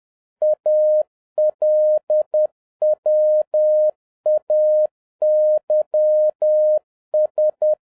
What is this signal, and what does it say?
Morse code, heard as a keyed tone, about 10 wpm, 610 Hz. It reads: ALWAYS
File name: 2011-07-mp3 / n56 BACK FROM A BIG ACCIDENT 10wpm